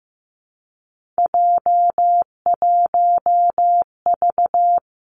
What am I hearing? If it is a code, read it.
Morse code, heard as a keyed tone, about 15 wpm, 700 Hz. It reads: J1V
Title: Three Challenging Letter-Number Combinations 15wpm